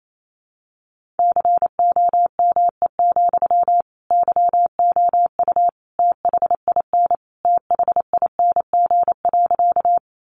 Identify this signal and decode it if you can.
Morse code, heard as a keyed tone, about 28 wpm, 700 Hz. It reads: COME, YOU THIN THING.
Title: Sentences from Top 1200 Words 28wpm